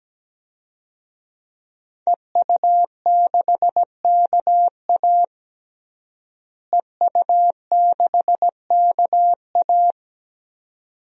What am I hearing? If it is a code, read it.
Morse code, heard as a keyed tone, about 17 wpm, 700 Hz. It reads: EU6KA EU6KA
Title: DXCC-100 Call Signs - 5 Characters - Repeated 2x - 17wpm